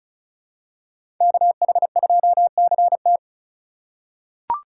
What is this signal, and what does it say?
Morse code, heard as a keyed tone, about 36 wpm, 700 Hz. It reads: KH2CT
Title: USA Call Signs - 5 Characters - 4x Character Spacing - 35wpm